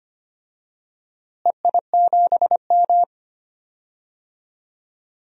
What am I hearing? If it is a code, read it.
Morse code, heard as a keyed tone, about 25 wpm, 700 Hz. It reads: EI7M
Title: DXCC-100 Call Signs - 4 Characters - Repeated 2x - 25wpm